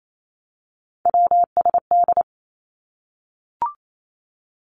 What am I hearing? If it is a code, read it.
Morse code, heard as a keyed tone, about 28 wpm, 700 Hz. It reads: WSD
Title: Three Challenging Letter-Number Combinations 28wpm